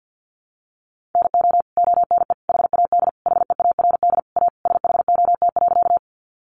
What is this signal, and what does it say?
Morse code, heard as a keyed tone, about 50 wpm, 700 Hz. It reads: NO ONE HAD HEARD A SHOT.
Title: Sentences from Top 1100 Words 50wpm